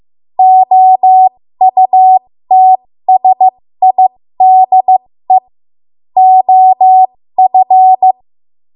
Morse code, heard as a keyed tone, about 15 wpm, spaced as for 13 wpm, 750 Hz. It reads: OUTSIDE OF
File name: Morse / 13WPM